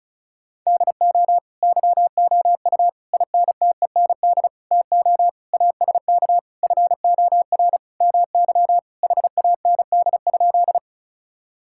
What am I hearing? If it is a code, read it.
Morse code, heard as a keyed tone, about 36 wpm, 700 Hz. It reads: DO YOU INTEND TO ASK FOR MY HAND?
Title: Sentences from Top 1600 Words 35wpm